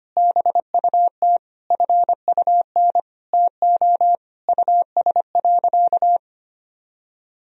Morse code, heard as a keyed tone, about 25 wpm, 700 Hz. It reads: BUT FUN TO US.
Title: Sentences from Top 1000 Words - Review Part 4 - 25wpm